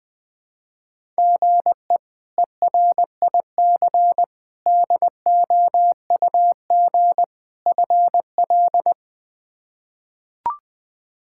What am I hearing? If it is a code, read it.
Morse code, heard as a keyed tone, about 20 wpm, 700 Hz. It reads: GE ERIC DOUG FL
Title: K1USN SST QSO Practice - 20wpm